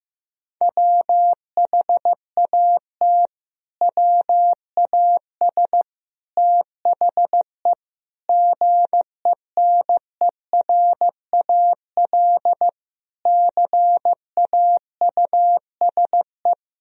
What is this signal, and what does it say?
Morse code, heard as a keyed tone, about 16 wpm, 700 Hz. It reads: WHAT WAS THE GENERAL CAUSE
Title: Sentences from Top 1000 Words - Review Part 3 - 15wpm